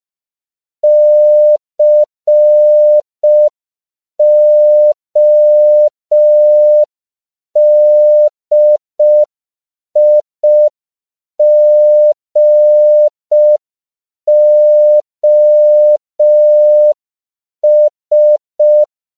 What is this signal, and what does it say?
Morse code, heard as a keyed tone, about 5 wpm, 600 Hz. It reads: CODIGOS